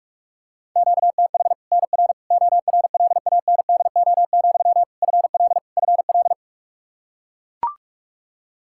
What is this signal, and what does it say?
Morse code, heard as a keyed tone, about 41 wpm, 700 Hz. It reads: QTH NR ORLANDO, FL FL
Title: Practice QSO - Exchange 1 - Beginning - 45wpm